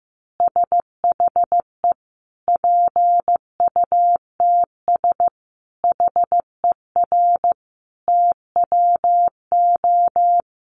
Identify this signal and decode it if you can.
Morse code, heard as a keyed tone, about 15 wpm, 700 Hz. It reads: SHE PUTS HER TWO
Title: Easy Phrases 15wpm